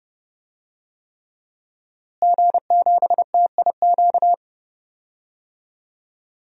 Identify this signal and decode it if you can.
Morse code, heard as a keyed tone, about 30 wpm, 700 Hz. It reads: G7TIQ
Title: DXCC-100 Call Signs - 5 Characters - Repeated 3x - 30wpm